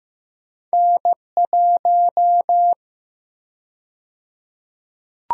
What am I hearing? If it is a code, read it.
Morse code, heard as a keyed tone, about 15 wpm, 700 Hz. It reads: N1
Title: USA Call Sign Prefixes - 15wpm